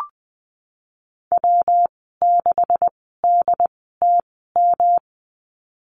Morse code, heard as a keyed tone, about 20 wpm, spaced as for 15 wpm, 700 Hz. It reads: W6DTM
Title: 173 Lesson 28g - Number 9 - Call Signs 20wpm